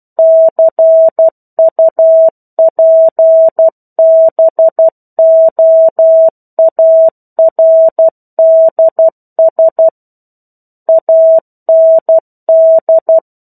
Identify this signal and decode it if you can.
Morse code, heard as a keyed tone, about 12 wpm, 650 Hz. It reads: CUPBOARDS AND
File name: 01-Chapter-01 12 WPM